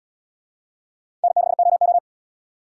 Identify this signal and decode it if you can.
Morse code, heard as a keyed tone, about 69 wpm, 700 Hz. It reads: N6OO